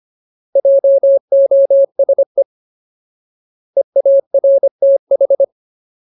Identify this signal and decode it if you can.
Morse code, heard as a keyed tone, about 25 wpm, 550 Hz. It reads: JOSE EARTH
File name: Ultimate Word List / 25WPM Ultimate